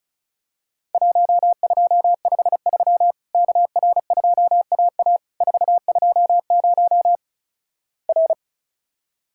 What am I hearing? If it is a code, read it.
Morse code, heard as a keyed tone, about 34 wpm, 700 Hz. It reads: 1253 KR2AA 420 R
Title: Worked All Europe DX Contest - QTC Exchange - 35wpm